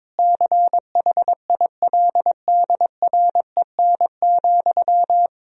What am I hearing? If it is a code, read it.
Morse code, heard as a keyed tone, about 21 wpm, 700 Hz. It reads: CHILDREN,
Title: Sentences from Top 1000 Words - Review Part 3 - 22wpm